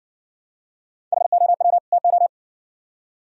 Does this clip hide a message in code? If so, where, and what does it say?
Yes: Morse code from 1.1 to 2.3 s, heard as a keyed tone, about 57 wpm, 700 Hz. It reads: HOW TO